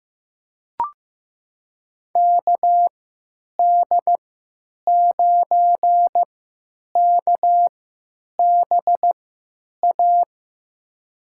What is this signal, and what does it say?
Morse code, heard as a keyed tone, about 15 wpm, spaced as for 9 wpm, 700 Hz. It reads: KD9KBA
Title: USA Call Signs - 6 Characters - 3x Character Spacing - 15wpm